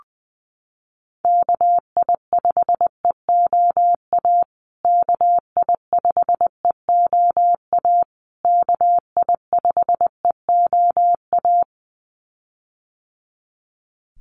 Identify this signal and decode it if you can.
Morse code, heard as a keyed tone, about 20 wpm, 700 Hz. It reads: KI5EOA KI5EOA KI5EOA
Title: USA Call Signs Repeated 3x 20wpm